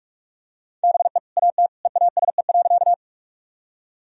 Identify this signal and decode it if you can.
Morse code, heard as a keyed tone, about 46 wpm, 700 Hz. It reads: BE AT EASE.
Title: Sentences from Top 1000 Words - Review Part 1 - 45wpm